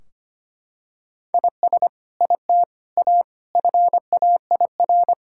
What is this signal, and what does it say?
Morse code, heard as a keyed tone, about 25 wpm, 700 Hz. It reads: IS IT A FAIR
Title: Sentences from Top 1600 Words 25wpm